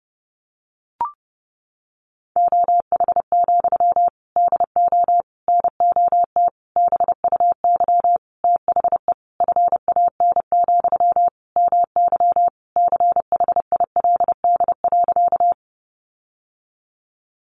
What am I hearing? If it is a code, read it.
Morse code, heard as a keyed tone, about 30 wpm, 700 Hz. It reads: OH, DO NOT BUY THE FAN, MY CHILD.
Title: Sentences from Top 1300 Words 30wpm